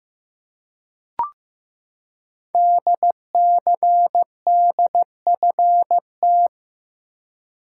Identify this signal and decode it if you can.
Morse code, heard as a keyed tone, about 15 wpm, 700 Hz. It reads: DCDFT
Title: Code Groups - 5 Letters - 15wpm